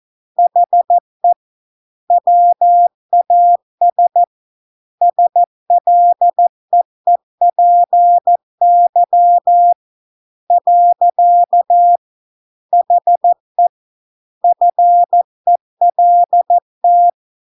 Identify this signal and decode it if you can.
Morse code, heard as a keyed tone, about 14 wpm, 700 Hz. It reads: HE WAS SLEEPY . HE FELT